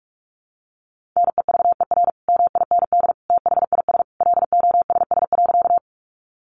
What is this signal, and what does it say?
Morse code, heard as a keyed tone, about 44 wpm, 700 Hz. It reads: NEVER MIND THIS LOSS.
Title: Sentences from Top 1400 Words 45wpm